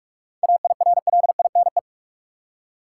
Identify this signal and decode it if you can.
Morse code, heard as a keyed tone, about 47 wpm, 700 Hz. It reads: AIRLINE
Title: Seven Letter Words - 45wpm